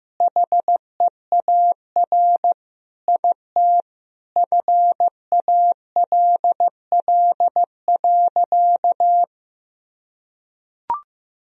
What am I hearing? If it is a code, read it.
Morse code, heard as a keyed tone, about 15 wpm, 700 Hz. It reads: HEAR IT FALL.
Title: Sentences from Top 1000 Words - Review Part 2 - 15wpm